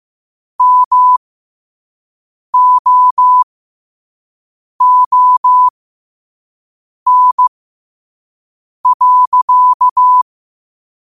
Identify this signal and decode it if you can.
Morse code, heard as a keyed tone, about 15 wpm, spaced as for 5 wpm, 1000 Hz. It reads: MOON.
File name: qotdfiles / 5wpm